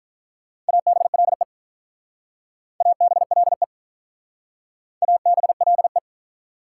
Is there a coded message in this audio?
Yes: Morse code from 0.7 to 6.0 s, heard as a keyed tone, about 45 wpm, 700 Hz. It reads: ABLE ABLE ABLE